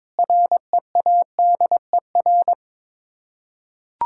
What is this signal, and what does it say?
Morse code, heard as a keyed tone, about 22 wpm, 700 Hz. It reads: READER
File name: Difficult Words with Hints - 22wpm